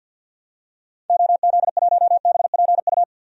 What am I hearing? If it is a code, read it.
Morse code, heard as a keyed tone, about 52 wpm, 700 Hz. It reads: OZ1BPU
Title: DXCC-100 Call Signs - 6 Characters - 50wpm